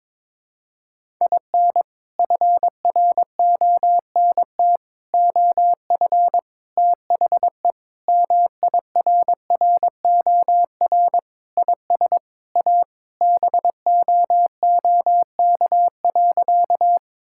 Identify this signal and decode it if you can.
Morse code, heard as a keyed tone, about 22 wpm, 700 Hz. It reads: IN FRONT OF THE MIRROR IS A BOOK.
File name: practice / 22wpm / Sentences with 4-Word Phrases - 22wpm